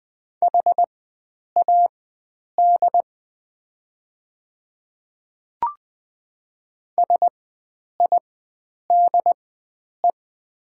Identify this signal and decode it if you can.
Morse code, heard as a keyed tone, about 20 wpm, spaced as for 9 wpm, 700 Hz. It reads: HAD SIDE
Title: 4x Character Spacing - Top 100 Words - 20wpm